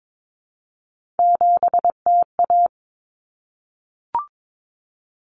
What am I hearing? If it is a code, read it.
Morse code, heard as a keyed tone, about 22 wpm, 700 Hz. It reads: 7TA
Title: Common Cut Numbers - 3-Digit 22wpm